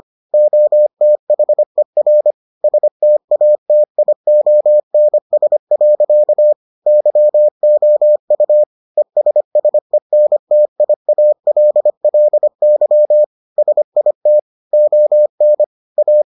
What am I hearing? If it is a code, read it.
Morse code, heard as a keyed tone, about 25 wpm, 600 Hz. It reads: OTHER STATIONS. YOU ESSENTIALLY SIT ON A